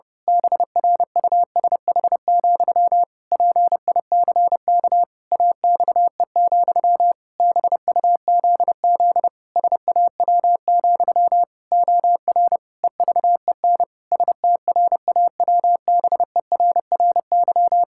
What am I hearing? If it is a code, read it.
Morse code, heard as a keyed tone, about 30 wpm, 700 Hz. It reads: BRUSH, PICK AXE, BUZZ SAW, OR EVEN STRAWBERRY
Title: Pangram Challenge v2 - 30wpm